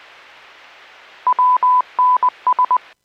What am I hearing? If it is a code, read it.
Morse code, heard as a keyed tone, about 20 wpm, 1000 Hz. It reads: WNS